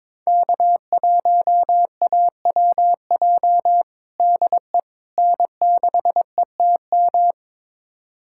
Morse code, heard as a keyed tone, about 22 wpm, 700 Hz. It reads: K1AWJ DE N6ETM
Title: USA Callsign DE Callsign 22wpm